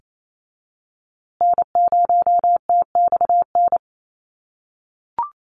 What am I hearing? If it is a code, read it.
Morse code, heard as a keyed tone, about 28 wpm, 700 Hz. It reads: N0TXN
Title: USA Call Signs Repeated 2x 28wpm